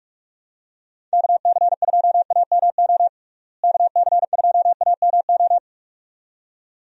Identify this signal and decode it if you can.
Morse code, heard as a keyed tone, about 46 wpm, 700 Hz. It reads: KC2AMO KC2AMO
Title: USA Call Signs - 6 Characters - Repeated 2x - 45wpm